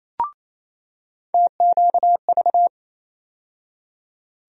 Morse code, heard as a keyed tone, about 28 wpm, 700 Hz. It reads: TQV
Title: USA Call Sign Postfixes - 3 Letters 28wpm